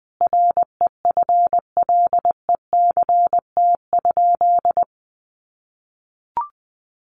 Morse code, heard as a keyed tone, about 19 wpm, 700 Hz. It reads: REFLECT?
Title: Sentences from Top 1500 Words 20wpm